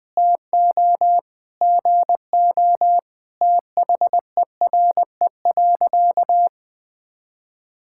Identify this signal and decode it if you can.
Morse code, heard as a keyed tone, about 21 wpm, 700 Hz. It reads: TO GO THERE.
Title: Sentences with 5-Word Phrases - 20wpm